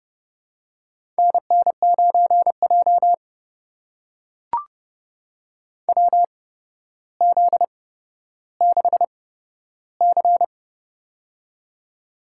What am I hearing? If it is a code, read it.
Morse code, heard as a keyed tone, about 30 wpm, spaced as for 13 wpm, 700 Hz. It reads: NN9J WZ6C
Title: USA Call Signs - 4 Characters - 8x Character Spacing - 30wpm